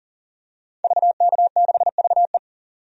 Morse code, heard as a keyed tone, about 40 wpm, 700 Hz. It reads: VK6VE
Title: DXCC-100 Call Signs - 5 Characters - 40wpm